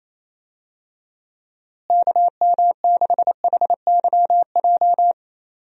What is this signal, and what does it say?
Morse code, heard as a keyed tone, about 28 wpm, 700 Hz. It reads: KM6HYJ
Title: USA Call Signs - 6 Characters - Repeated 3x - 28wpm